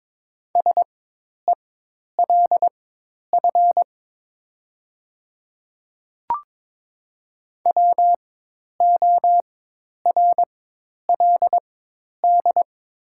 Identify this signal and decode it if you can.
Morse code, heard as a keyed tone, about 22 wpm, spaced as for 10 wpm, 700 Hz. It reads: SELF WORLD
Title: 4x Character Spacing - Top 200 Words - 22wpm